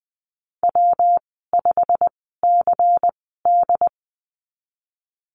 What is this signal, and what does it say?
Morse code, heard as a keyed tone, about 20 wpm, spaced as for 15 wpm, 700 Hz. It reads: W5CD
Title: 122 Lesson 20g - Number 3 - Call Signs 20wpm